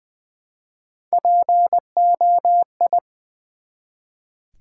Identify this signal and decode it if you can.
Morse code, heard as a keyed tone, about 20 wpm, 700 Hz. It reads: POI
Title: Build A Word - 1 Letter At A Time - 5-Letter Words - 20wpm